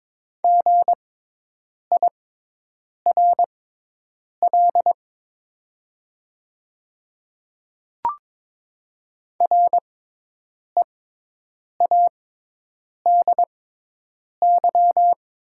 Morse code, heard as a keyed tone, about 23 wpm, spaced as for 8 wpm, 700 Hz. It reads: GIRL READY
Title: 6x Character Spacing - Top 500 Words - 22wpm